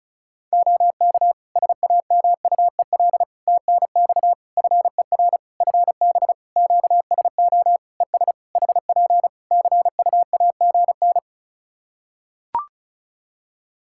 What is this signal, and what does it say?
Morse code, heard as a keyed tone, about 36 wpm, 700 Hz. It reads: OK SAMUEL TNX FER FB QSO ES HP CUAGN
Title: K1USN SST QSO Practice 35wpm